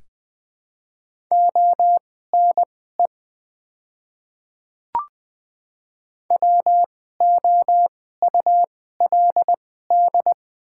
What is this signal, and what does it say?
Morse code, heard as a keyed tone, about 20 wpm, spaced as for 14 wpm, 700 Hz. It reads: ONE WOULD
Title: 2x Character Spacing - Top 100 Words - 20wpm